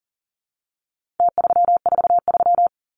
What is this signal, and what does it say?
Morse code, heard as a keyed tone, about 40 wpm, 700 Hz. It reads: T343
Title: Numbers - Contest Style - 4-Digit 40wpm